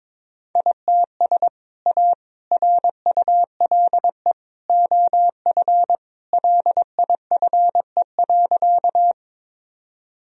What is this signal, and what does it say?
Morse code, heard as a keyed tone, about 22 wpm, 700 Hz. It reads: ITS A RULE OF LIFE.
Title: Sentences from Top 1000 Words - Review Part 2 - 22wpm